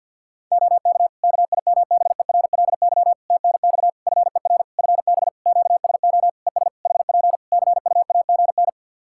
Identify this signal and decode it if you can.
Morse code, heard as a keyed tone, about 52 wpm, 700 Hz. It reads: OK KIMBERLY TNX FER FB QSO ES HP CUAGN
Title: Practice QSO - Exchange 4 - Ending - 50wpm